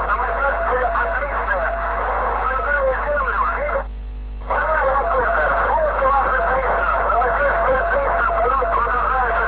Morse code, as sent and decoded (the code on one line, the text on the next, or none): none